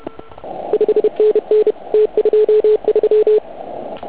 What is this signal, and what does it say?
Morse code, heard as a keyed tone, about 31 wpm, 430 Hz. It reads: 5NN T23
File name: Archiv / CUC149 / OK1IF CUC149